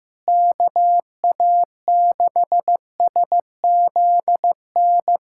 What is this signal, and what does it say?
Morse code, heard as a keyed tone, about 15 wpm, 700 Hz. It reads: KA6SZN
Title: USA Callsign DE Callsign - 6 Characters - 15wpm